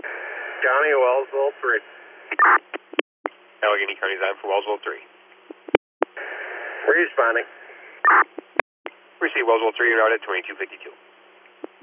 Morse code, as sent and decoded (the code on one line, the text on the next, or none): none